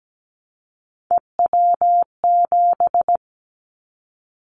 Morse code, heard as a keyed tone, about 17 wpm, 700 Hz. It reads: EW7
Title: Callsign Prefixes for Top 50 DXCC - 17wpm